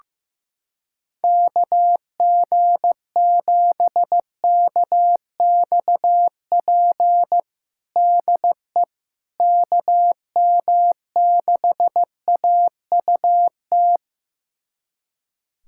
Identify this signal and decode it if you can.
Morse code, heard as a keyed tone, about 15 wpm, 700 Hz. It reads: KG7KXP DE KM6AUT
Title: USA Callsign DE Callsign - 6 Characters - 15wpm